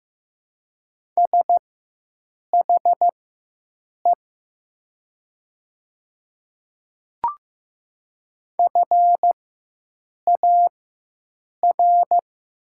Morse code, heard as a keyed tone, about 15 wpm, spaced as for 7 wpm, 700 Hz. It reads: SHE FAR
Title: 4x Character Spacing - Top 300 Words - 15wpm